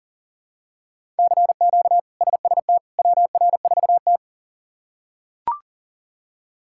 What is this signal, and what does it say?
Morse code, heard as a keyed tone, about 37 wpm, 700 Hz. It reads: CQ SST WR4T
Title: K1USN SST QSO Practice - 40wpm